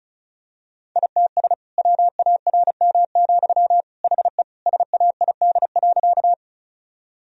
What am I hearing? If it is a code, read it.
Morse code, heard as a keyed tone, about 34 wpm, 700 Hz. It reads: ITS WARM, HE SAID.
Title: Sentences from Top 1000 Words - Review Part 2 - 35wpm